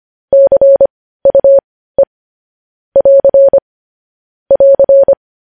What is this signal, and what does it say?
Morse code, heard as a keyed tone, about 25 wpm, spaced as for 15 wpm, 550 Hz. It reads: CUE <AR> <AR>